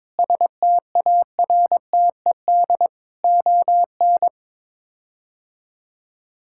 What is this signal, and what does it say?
Morse code, heard as a keyed tone, about 22 wpm, 700 Hz. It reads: STARTED ON
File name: practice / 22wpm / Sets of 7 Words - 22wpm